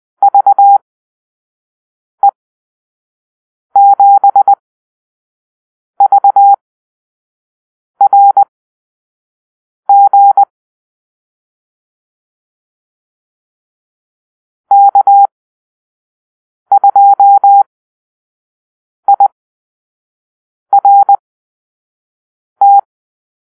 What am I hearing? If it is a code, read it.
Morse code, heard as a keyed tone, about 20 wpm, spaced as for 5 wpm, 800 Hz. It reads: VE7VRG K2IRT
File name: arrl / Call-Sign Drill-1